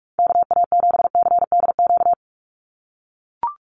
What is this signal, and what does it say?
Morse code, heard as a keyed tone, about 44 wpm, 700 Hz. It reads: KA7CDQ
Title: USA Call Signs 45wpm